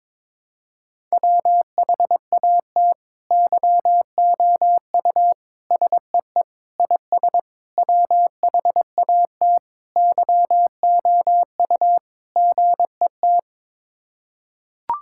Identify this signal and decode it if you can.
Morse code, heard as a keyed tone, about 22 wpm, 700 Hz. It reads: WHAT YOU SEE IS WHAT YOU GET
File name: practice / 22wpm / Sets of 7 Words - 22wpm